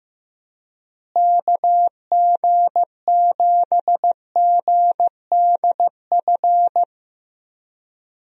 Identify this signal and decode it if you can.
Morse code, heard as a keyed tone, about 15 wpm, 700 Hz. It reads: KG7GDF
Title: USA Call Signs - 6 Characters - Repeated 3x - 15wpm